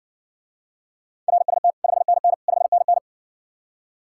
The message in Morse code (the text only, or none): RST 5NN 5NN